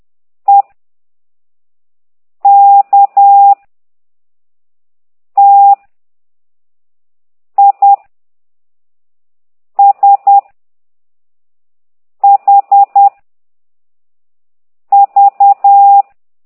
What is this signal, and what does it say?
Morse code, heard as a keyed tone, about 10 wpm, 800 Hz. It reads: E K T I S H V
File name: ct2idw / exerc9